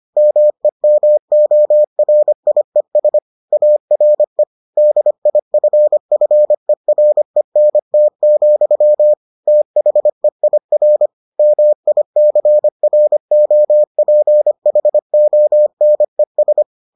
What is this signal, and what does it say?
Morse code, heard as a keyed tone, about 25 wpm, 600 Hz. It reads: MEMORIES ARE DIFFERENT, THEIR MICROPHONES